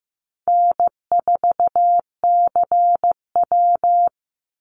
Morse code, heard as a keyed tone, about 15 wpm, 700 Hz. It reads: N4CW
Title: Call Signs Repeated 2x - 15wpm